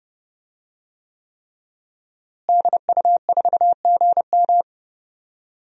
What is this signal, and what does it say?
Morse code, heard as a keyed tone, about 30 wpm, 700 Hz. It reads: DU4GM
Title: DXCC-100 Call Signs - 5 Characters - Repeated 3x - 30wpm